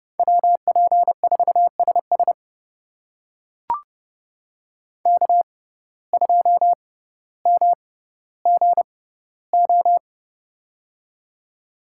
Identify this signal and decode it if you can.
Morse code, heard as a keyed tone, about 31 wpm, spaced as for 16 wpm, 700 Hz. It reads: WP4SS K2MGO
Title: USA Call Signs - 5 Characters - 6x Character Spacing - 30wpm